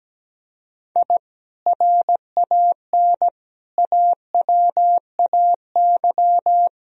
Morse code, heard as a keyed tone, about 17 wpm, 700 Hz. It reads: I RAN AWAY
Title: Sentences from Top 1000 Words - Review Part 3 - 17wpm